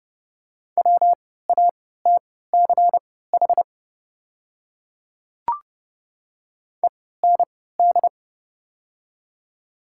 Morse code, heard as a keyed tone, about 30 wpm, spaced as for 17 wpm, 700 Hz. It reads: WATCH END
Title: 3x Character Spacing - Top 300 Words - 30wpm